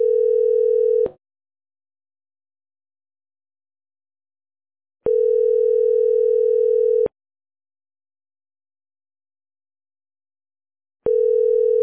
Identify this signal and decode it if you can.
Ringback tone (North America), 440 + 480 Hz.